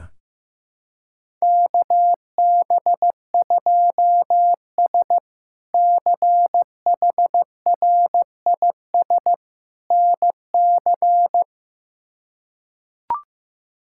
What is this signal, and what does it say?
Morse code, heard as a keyed tone, about 15 wpm, 700 Hz. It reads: KB2S CHRIS NC
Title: K1USN SST QSO Practice - 15wpm